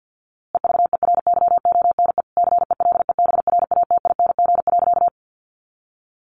Morse code, heard as a keyed tone, about 50 wpm, 700 Hz. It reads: EVERYONE CELEBRATING.
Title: A Sound Of Thunder 50wpm